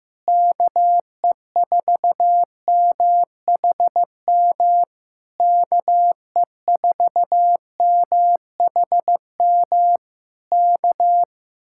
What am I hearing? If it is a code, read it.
Morse code, heard as a keyed tone, about 15 wpm, 700 Hz. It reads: KE4MHM KE4MHM K
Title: Speed-Racing Common QSO Phrases Encore - 15wpm